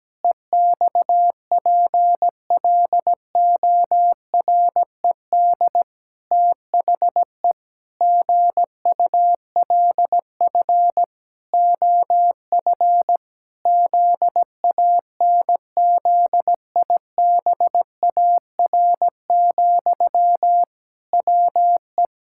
Morse code, heard as a keyed tone, about 17 wpm, 700 Hz. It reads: EXPLORED THE GULF OF ZANZIBAR, WE
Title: Pangram Challenge v2 - 17wpm